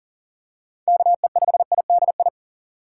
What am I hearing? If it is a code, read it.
Morse code, heard as a keyed tone, about 40 wpm, 700 Hz. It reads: KE5IDI
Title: USA Call Signs - 2x Character Spacing - 40wpm